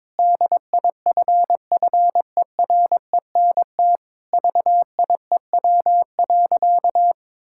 Morse code, heard as a keyed tone, about 21 wpm, 700 Hz. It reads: DIFFERENT VIEW.
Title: Sentences from Top 1600 Words 22wpm